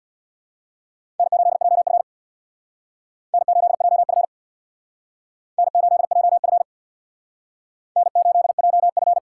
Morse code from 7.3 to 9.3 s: N8JF

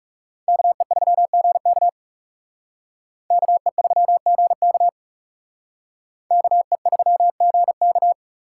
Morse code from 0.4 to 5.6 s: KE3GK KE3GK

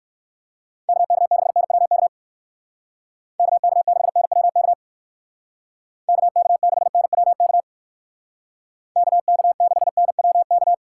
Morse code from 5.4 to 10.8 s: KK6NWK KK6NWK

